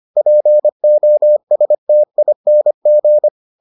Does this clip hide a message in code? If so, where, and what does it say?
Yes: Morse code from 0.2 to 3.3 s, heard as a keyed tone, about 25 wpm, 600 Hz. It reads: POSTING